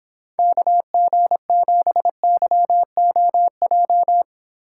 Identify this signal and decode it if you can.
Morse code, heard as a keyed tone, about 26 wpm, 700 Hz. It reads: KG7YOJ